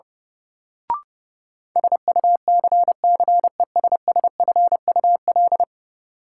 Morse code, heard as a keyed tone, about 32 wpm, 700 Hz. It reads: SUCCESSFUL
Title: Sentences from Top 900 Words - New Word Drill - 30wpm